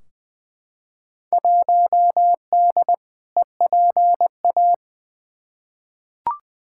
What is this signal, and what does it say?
Morse code, heard as a keyed tone, about 20 wpm, 700 Hz. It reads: 1D EPA
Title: Field Day Exchanges - 20wpm